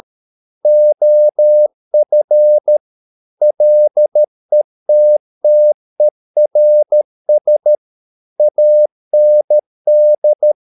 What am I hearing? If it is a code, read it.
Morse code, heard as a keyed tone, about 13 wpm, 600 Hz. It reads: OF LETTERS AND